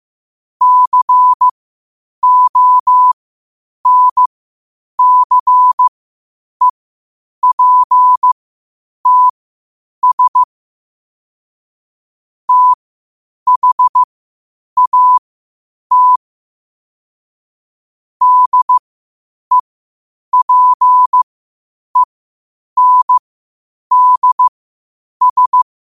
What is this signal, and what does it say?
Morse code, heard as a keyed tone, about 15 wpm, spaced as for 8 wpm, 1000 Hz. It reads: CONCEPTS THAT DEPENDS